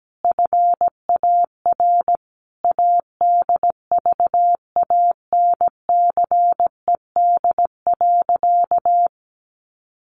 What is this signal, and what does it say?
Morse code, heard as a keyed tone, about 17 wpm, 700 Hz. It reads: FAR ADVANCED.
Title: Sentences from Top 1400 Words 17wpm